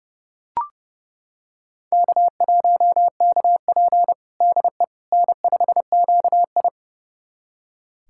Morse code, heard as a keyed tone, about 30 wpm, 700 Hz. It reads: K1KP DE N5QI
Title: USA Callsign DE Callsign - 4 Characters - 30wpm